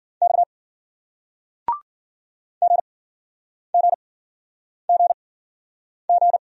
Morse code, heard as a keyed tone, about 54 wpm, spaced as for 9 wpm, 700 Hz. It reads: X GGGG